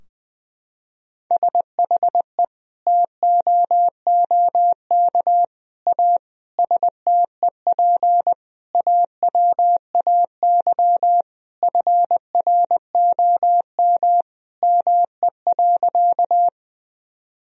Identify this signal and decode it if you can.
Morse code, heard as a keyed tone, about 21 wpm, 700 Hz. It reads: SHE TOOK A STEP AWAY FROM ME.
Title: Sentences from Top 1000 Words - Review Part 2 - 20wpm